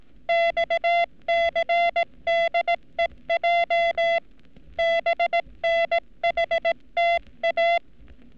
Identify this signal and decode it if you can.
Morse code, heard as a keyed tone, about 18 wpm, 680 Hz. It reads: XCDEJ BNHTA
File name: training / NAVPERS 11224-11235 / NAVPERS 11233 RA RB